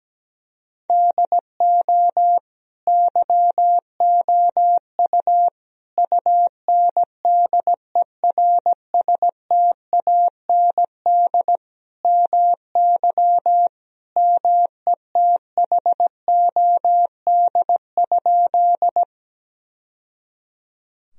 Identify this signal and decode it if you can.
Morse code, heard as a keyed tone, about 17 wpm, 700 Hz. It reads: DO YOU UNDERSTAND MY METHOD?
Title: Sentences from Top 1000 Words - Review Part 3 - 17wpm